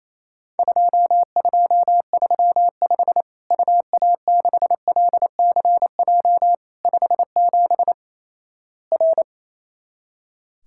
Morse code, heard as a keyed tone, about 28 wpm, 700 Hz. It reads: 2235 UA6LCJ 57 R
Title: Worked All Europe DX Contest - QTC Exchange - 28wpm